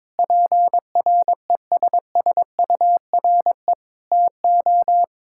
Morse code, heard as a keyed tone, about 22 wpm, 700 Hz. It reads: PRESSURE TO
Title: Sentences from Top 1000 Words - Review Part 6 - 22wpm